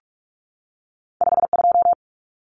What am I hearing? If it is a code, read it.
Morse code, heard as a keyed tone, about 46 wpm, 700 Hz. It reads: 52
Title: Numbers - 2-Digit 45wpm